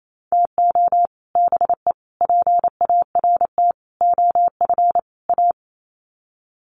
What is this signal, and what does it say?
Morse code, heard as a keyed tone, about 28 wpm, 700 Hz. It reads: TO BE PART OF A
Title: Sets of 5 Words Encore2 28wpm